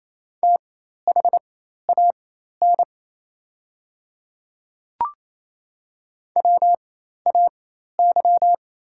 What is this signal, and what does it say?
Morse code, heard as a keyed tone, about 28 wpm, spaced as for 13 wpm, 690 Hz. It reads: THAN WAY
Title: 4x Character Spacing - Top 100 Words - 28wpm